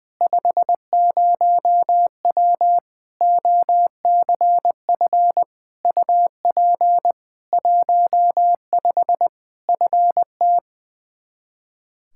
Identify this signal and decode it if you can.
Morse code, heard as a keyed tone, about 21 wpm, 700 Hz. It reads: 50W OCF UP 15 FT
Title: Practice QSO - Exchange 2 - Info - 20wpm